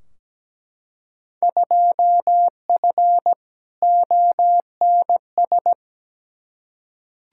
Morse code, heard as a keyed tone, about 17 wpm, 700 Hz. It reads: 2F ONS
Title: Field Day Exchanges - 17wpm